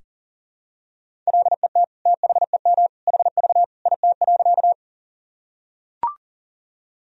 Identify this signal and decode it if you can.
Morse code, heard as a keyed tone, about 37 wpm, 700 Hz. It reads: LET THEM HV IT.
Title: Sentences from Top 300 Words and Abbreviations 40wpm